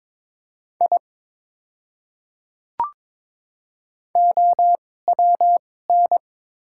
Morse code, heard as a keyed tone, about 22 wpm, spaced as for 16 wpm, 690 Hz. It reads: I OWN
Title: 2x Character Spacing - Top 500 Words - 22wpm